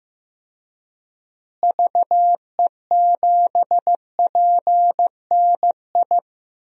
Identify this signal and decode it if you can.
Morse code, heard as a keyed tone, about 15 wpm, 700 Hz. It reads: VE7PNI
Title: DXCC-100 Call Signs - 6 Characters - Repeated 3x - 15wpm